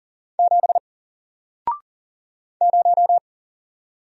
Morse code, heard as a keyed tone, about 38 wpm, 700 Hz. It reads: 7 0
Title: Single Letter-Number 40wpm